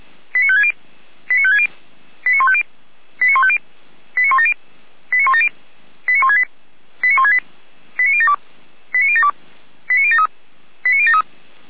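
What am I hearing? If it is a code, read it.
Five-tone selective call (ZVEI), heard as quick runs of five tones, tones of about 70 ms. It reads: 89560 89560 89150 89150 89170 89170 89168 89168 89072 89072 89073 89073
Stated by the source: ZVEI five-tone selective call 89167 received via SDR, from the Feuerwehr Orlamünde server.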